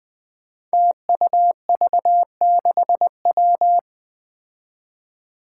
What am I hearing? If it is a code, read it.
Morse code, heard as a keyed tone, about 20 wpm, 700 Hz. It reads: TUV6W
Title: Five Challenging Letter-Number Combinations 20wpm